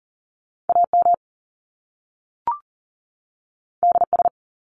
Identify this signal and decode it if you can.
Morse code, heard as a keyed tone, about 40 wpm, 700 Hz. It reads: AM DS
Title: Top Two Letter Combinations in Words 40wpm